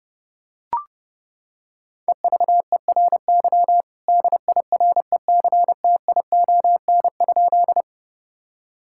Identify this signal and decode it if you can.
Morse code, heard as a keyed tone, about 32 wpm, 700 Hz. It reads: EVERY DIRECTION?
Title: Sentences from Top 1000 Words - Review Part 5 - 30wpm